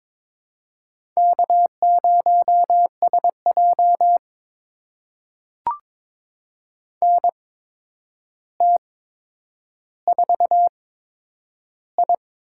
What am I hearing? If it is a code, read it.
Morse code, heard as a keyed tone, about 21 wpm, spaced as for 9 wpm, 700 Hz. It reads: K0SJ NT4I
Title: USA Call Signs - 4 Characters - 8x Character Spacing - 22wpm